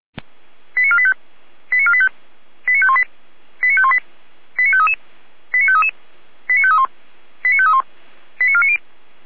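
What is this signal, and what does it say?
Five-tone selective call (ZVEI), heard as quick runs of five tones, tones of about 70 ms. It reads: 89485 89485 89518 89518 89533 89533 89531 89531 89490
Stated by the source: ZVEI five-tone selective call received by SDR, channel 455.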